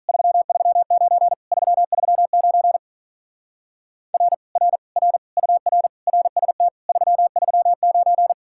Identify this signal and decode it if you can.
Morse code, heard as a keyed tone, about 42 wpm, 700 Hz. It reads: 339 339 R R R UR RST 339
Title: Speed-Racing Common QSO Phrases Encore - 35wpm